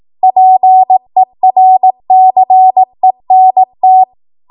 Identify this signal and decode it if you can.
Morse code, heard as a keyed tone, about 18 wpm, 750 Hz. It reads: PERCENT
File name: Archive / 18 WPM / 18WPM